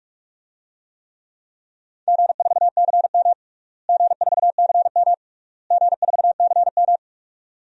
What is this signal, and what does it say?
Morse code, heard as a keyed tone, about 46 wpm, 700 Hz. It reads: G4CM G4CM G4CM
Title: DXCC-100 Call Signs - 4 Characters - Repeated 3x - 45wpm